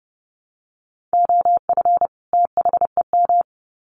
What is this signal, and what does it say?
Morse code, heard as a keyed tone, about 30 wpm, 700 Hz. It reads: OF THEM
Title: Sets of 2 Words 30wpm